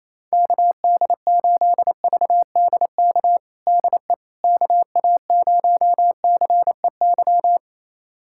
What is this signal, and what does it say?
Morse code, heard as a keyed tone, about 28 wpm, 700 Hz. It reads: KD8VDK DE KA0CEY
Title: USA Callsign DE Callsign 28wpm